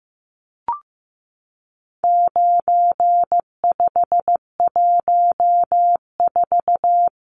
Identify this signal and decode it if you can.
Morse code, heard as a keyed tone, about 15 wpm, 700 Hz. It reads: 9514
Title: Numbers - 4-Digit 15wpm